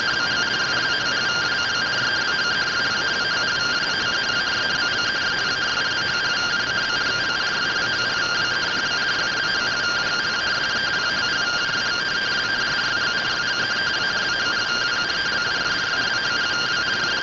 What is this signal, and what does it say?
Two-tone teleprinter signal, about 170 Hz shift.
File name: findings / CORONAL fr69 nb202 3mm.gen-38 rtty rx